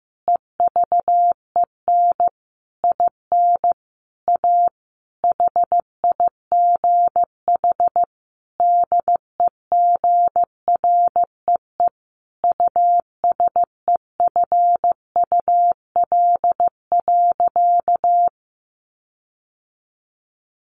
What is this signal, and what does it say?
Morse code, heard as a keyed tone, about 15 wpm, 700 Hz. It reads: EVEN IN A HIGH DEGREE USEFUL.
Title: Sentences from Top 1500 Words 15wpm